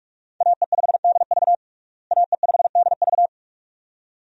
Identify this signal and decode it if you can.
Morse code, heard as a keyed tone, about 46 wpm, 700 Hz. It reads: AE5DV AE5DV
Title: USA Call Signs - 5 Characters - Repeated 2x - 45wpm